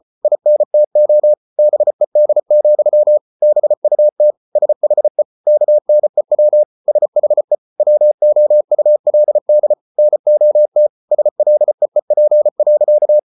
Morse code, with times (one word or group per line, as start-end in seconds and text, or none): 0.2-1.3 INTO
1.6-3.2 BED,
3.4-4.3 BUT
4.6-5.2 SHE
5.5-6.6 KNEW
6.9-7.5 SHE
7.8-9.7 WOULD
10.0-10.9 NOT
11.1-13.2 SLEEP.